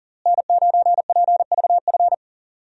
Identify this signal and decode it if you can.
Morse code, heard as a keyed tone, about 40 wpm, 700 Hz. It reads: N9PVF
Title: USA Call Signs - 5 Characters - Repeated 2x - 40wpm